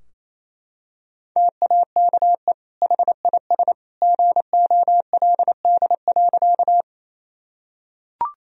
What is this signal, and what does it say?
Morse code, heard as a keyed tone, about 28 wpm, 700 Hz. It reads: TAKE HIS GOLD.
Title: Sentences from Top 1000 Words - Review Part 3 - 28wpm